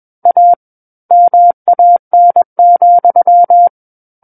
Morse code, heard as a keyed tone, about 21 wpm, 700 Hz. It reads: A MAN,
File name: Book One / 12-Chapter 12 - 21 WPM